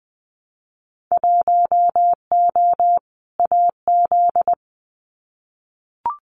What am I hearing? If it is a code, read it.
Morse code, heard as a keyed tone, about 20 wpm, 700 Hz. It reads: 1O AZ
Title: Winter Field Day Exchanges 20wpm